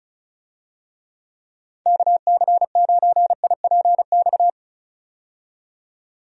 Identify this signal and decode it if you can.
Morse code, heard as a keyed tone, about 36 wpm, 700 Hz. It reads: KC9IPX
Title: USA Call Signs - 6 Characters - 35wpm